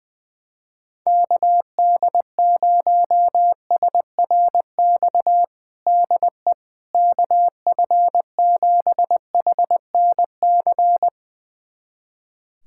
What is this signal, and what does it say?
Morse code, heard as a keyed tone, about 20 wpm, 700 Hz. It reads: KD0SRX DE KF7HNC
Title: USA Callsign DE Callsign - 6 Characters - 20wpm